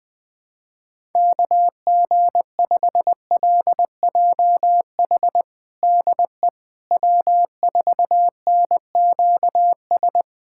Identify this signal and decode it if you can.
Morse code, heard as a keyed tone, about 20 wpm, 700 Hz. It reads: KG5LJH DE W4NQS
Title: USA Callsign DE Callsign 20wpm